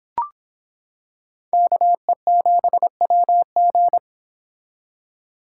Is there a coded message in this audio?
Yes: Morse code from 1.5 to 4.0 s, heard as a keyed tone, about 26 wpm, 700 Hz. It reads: KE7WG